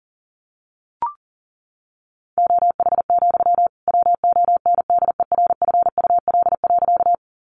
Morse code, heard as a keyed tone, about 40 wpm, 700 Hz. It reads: OH, WONDERFUL.
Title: Sentences from Top 1100 Words 40wpm